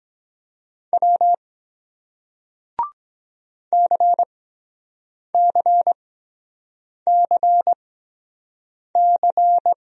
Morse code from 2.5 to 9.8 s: CCCC

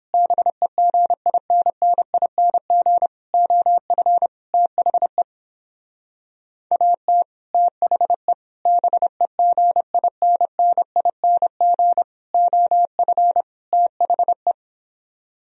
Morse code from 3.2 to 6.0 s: OF THE

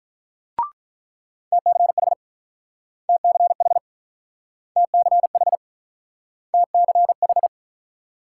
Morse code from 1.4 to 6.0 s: TCH TCH TCH

